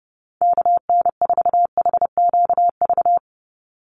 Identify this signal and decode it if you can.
Morse code, heard as a keyed tone, about 30 wpm, 700 Hz. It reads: KN4HQV
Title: USA Call Signs 30wpm